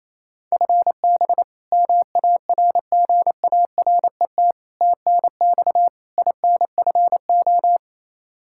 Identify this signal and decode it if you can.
Morse code, heard as a keyed tone, about 28 wpm, 700 Hz. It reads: FB MARGARET TNX INFO
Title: Practice QSO - Exchange 3 - More Info - 28wpm